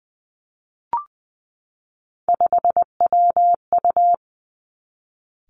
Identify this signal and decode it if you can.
Morse code, heard as a keyed tone, about 20 wpm, 700 Hz. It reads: 5WU
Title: Three Challenging Letter-Number Combinations 20wpm